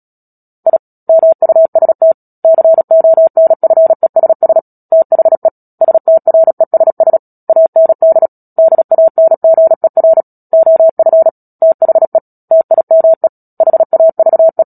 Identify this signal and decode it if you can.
Morse code, heard as a keyed tone, about 37 wpm, 650 Hz. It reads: I MUST CONFESS THE STRESS AND DANGER OF THE TIME HAVE
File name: Book Two / 10-Chapter 10 - Epilogue - 36 WPM